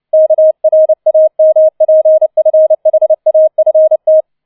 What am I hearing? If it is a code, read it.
Morse code, heard as a keyed tone, about 29 wpm, 630 Hz. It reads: KRAMPFHAFT